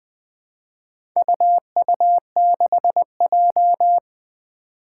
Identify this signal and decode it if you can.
Morse code, heard as a keyed tone, about 20 wpm, 700 Hz. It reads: UU6J
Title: Four Challenging Letter-Number Combinations 20wpm